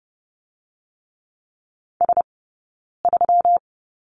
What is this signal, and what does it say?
Morse code, heard as a keyed tone, about 30 wpm, 700 Hz. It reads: S 3
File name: Lesson 20d - Number 3 - Two Characters 30wpm